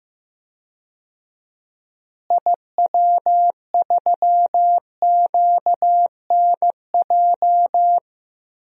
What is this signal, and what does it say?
Morse code, heard as a keyed tone, about 15 wpm, 700 Hz. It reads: IW3QNJ
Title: DXCC-100 Call Signs - 6 Characters - Repeated 3x - 15wpm